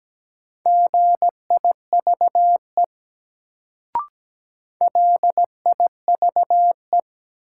Morse code, heard as a keyed tone, about 17 wpm, 700 Hz. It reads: GIVE LIVE
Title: Sentences from Top 200 Words - New Word Drill - 17wpm